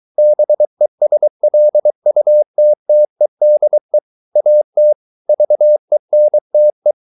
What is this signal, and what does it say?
Morse code, heard as a keyed tone, about 23 wpm, 600 Hz. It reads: BESLUTTEDE AT VENTE
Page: Ferie 23 wpm